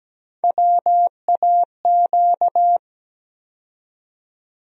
Morse code, heard as a keyed tone, about 17 wpm, 700 Hz. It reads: WAQ
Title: USA Call Sign Postfixes - 3 Letters 17wpm